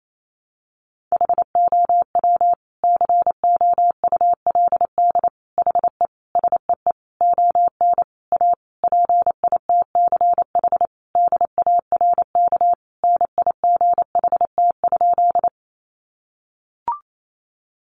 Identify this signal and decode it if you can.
Morse code, heard as a keyed tone, about 28 wpm, 700 Hz. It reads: HOW COULD HE SEE ON A PITCH DARK NIGHT?
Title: Sentences from Top 1300 Words 28wpm